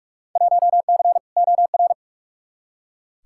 Morse code, heard as a keyed tone, about 46 wpm, 700 Hz. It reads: 1C OR
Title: Field Day Exchanges - 45wpm